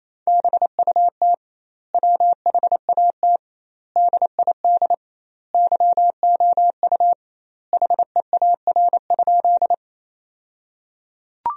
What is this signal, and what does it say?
Morse code, heard as a keyed tone, about 28 wpm, 700 Hz. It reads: BUT WHAT DID YOU HEAR?
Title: 2x Word-Spacing - Sentences from Top 300 Words - 28wpm